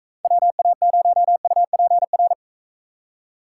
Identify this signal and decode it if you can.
Morse code, heard as a keyed tone, about 42 wpm, 700 Hz. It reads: WA0UPR